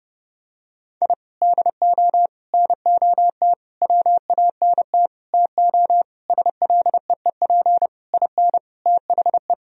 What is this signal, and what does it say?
Morse code, heard as a keyed tone, about 30 wpm, 700 Hz. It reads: I DO NOT WANT TO SLEEP IN THE
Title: Sentences from Top 1000 Words - Review Part 2 - 30wpm